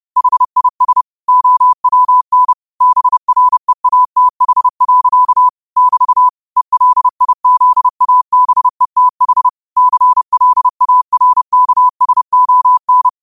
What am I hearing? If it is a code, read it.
Morse code, heard as a keyed tone, about 30 wpm, 1000 Hz. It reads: HIS OWN BREATH. <BT> ELIZABETH CLARKSON